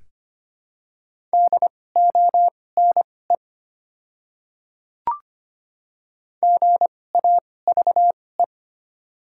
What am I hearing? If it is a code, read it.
Morse code, heard as a keyed tone, about 25 wpm, spaced as for 18 wpm, 700 Hz. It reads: DONE GAVE
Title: 2x Character Spacing - Top 500 Words - 25wpm